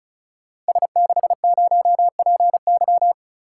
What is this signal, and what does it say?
Morse code, heard as a keyed tone, about 36 wpm, 700 Hz. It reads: S60PY
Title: Code Groups - 5 Letter:Numbers - 35wpm